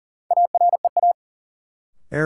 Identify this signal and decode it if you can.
Morse code, heard as a keyed tone, about 40 wpm, 700 Hz. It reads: AREA
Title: Sentences from Top 1000 Words - Review Part 2 - New Word Drill - 40wpm